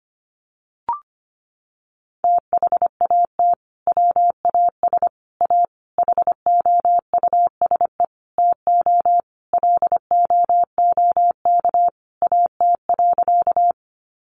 Morse code, heard as a keyed tone, about 25 wpm, 700 Hz. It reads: THAT WAS A HOUSE TO LOOK AT.
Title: Sentences from Top 200 Words 25wpm